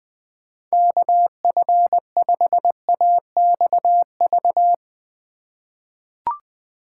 Morse code, heard as a keyed tone, about 20 wpm, 700 Hz. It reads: KF5AXV
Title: USA Call Signs - 6 Characters - 3x Character Spacing - 20wpm